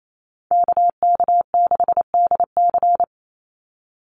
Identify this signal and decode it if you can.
Morse code, heard as a keyed tone, about 27 wpm, 700 Hz. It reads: KK6DC
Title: USA Call Signs 28wpm